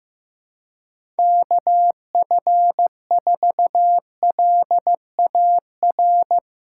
Morse code, heard as a keyed tone, about 16 wpm, 700 Hz. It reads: KF4LAR
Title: USA Callsign DE Callsign - 6 Characters - 15wpm